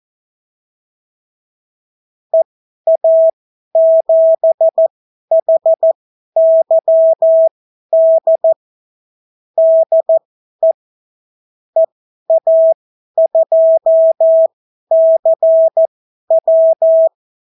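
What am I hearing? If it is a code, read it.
Morse code, heard as a keyed tone, about 14 wpm, spaced as for 11 wpm, 650 Hz. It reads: EA7HYD DE EA2CW
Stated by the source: Example Spanish CW QSO from a Morse learning book, 14 wpm characters with 8 wpm spacing.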